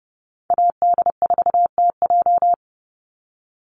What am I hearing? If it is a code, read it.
Morse code, heard as a keyed tone, about 29 wpm, 700 Hz. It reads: AD4TJ
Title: Call Signs - 30wpm